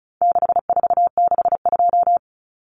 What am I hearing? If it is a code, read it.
Morse code, heard as a keyed tone, about 34 wpm, 700 Hz. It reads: 6462
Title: Numbers - 4-Digit 35wpm